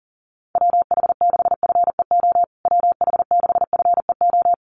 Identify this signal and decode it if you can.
Morse code, heard as a keyed tone, about 37 wpm, 700 Hz. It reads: WH6FEO WH6FEO
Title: USA Call Signs Repeated 3x 40wpm